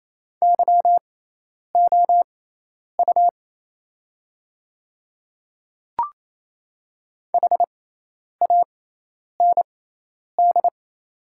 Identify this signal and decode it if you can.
Morse code, heard as a keyed tone, about 28 wpm, spaced as for 10 wpm, 700 Hz. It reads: YOU HAND
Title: 6x Character Spacing - Top 300 Words - 28wpm